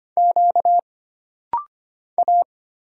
Morse code, heard as a keyed tone, about 25 wpm, 700 Hz. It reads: Q A
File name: Single Character (Letter:Number:Punctuation:Prosign) - Rapid-Fire - 25wpm